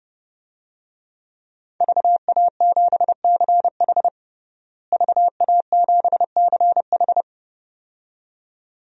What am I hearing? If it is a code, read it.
Morse code, heard as a keyed tone, about 30 wpm, 700 Hz. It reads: VA7CH VA7CH
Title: DXCC-100 Call Signs - 5 Characters - Repeated 2x - 30wpm